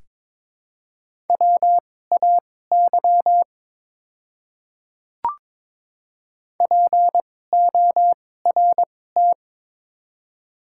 Morse code, heard as a keyed tone, about 22 wpm, spaced as for 16 wpm, 700 Hz. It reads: WAY PORT
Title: 2x Character Spacing - Top 200 Words - 22wpm